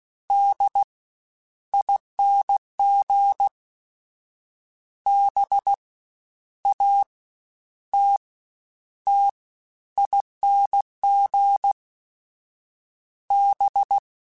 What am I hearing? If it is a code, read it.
Morse code, heard as a keyed tone, about 16 wpm, spaced as for 10 wpm, 800 Hz. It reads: DING BATTING B
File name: Special INGs-mp3s / Special INGs-mp3s / Fast ing @15